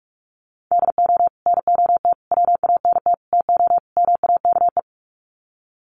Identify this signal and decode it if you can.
Morse code, heard as a keyed tone, about 44 wpm, 700 Hz. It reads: DO NOT WANT TO MAKE
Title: Sets of 5 Words Encore 45wpm